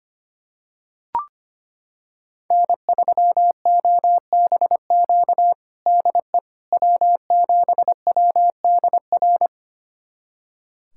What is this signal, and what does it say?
Morse code, heard as a keyed tone, about 25 wpm, 700 Hz. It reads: N3OBQ DE W7WDR
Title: USA Callsign DE Callsign - 5 Characters - 25wpm